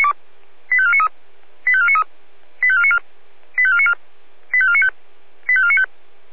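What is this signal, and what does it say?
Five-tone selective call (ZVEI), heard as quick runs of five tones, tones of about 70 ms. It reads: 85493 85493 85494 85494 85495 85495
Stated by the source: Five-tone ZVEI selective call received by SDR.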